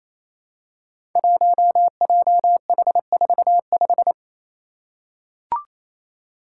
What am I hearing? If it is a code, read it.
Morse code, heard as a keyed tone, about 28 wpm, 700 Hz. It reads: 1JH45
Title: Five Challenging Letter-Number Combinations 28wpm